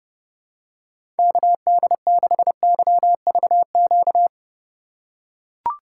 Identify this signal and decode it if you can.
Morse code, heard as a keyed tone, about 30 wpm, 700 Hz. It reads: KD6YVQ
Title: USA Call Signs - 6 Characters - 3x Character Spacing - 30wpm